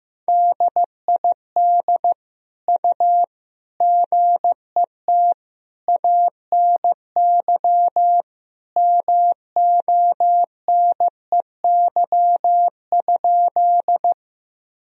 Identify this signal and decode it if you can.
Morse code, heard as a keyed tone, about 15 wpm, 700 Hz. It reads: DID U GET ANY MONEY?
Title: Sentences from Top 400 Words and Abbreviations 15wpm